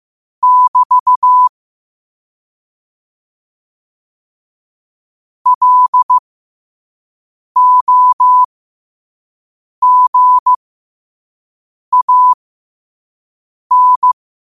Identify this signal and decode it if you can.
Morse code, heard as a keyed tone, about 15 wpm, spaced as for 5 wpm, 1000 Hz. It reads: <BT> LOGAN
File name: qotdfiles / 5wpm